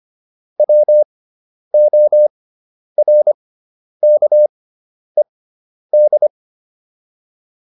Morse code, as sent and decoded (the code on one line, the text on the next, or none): .-- --- .-. -.- . -..
WORKED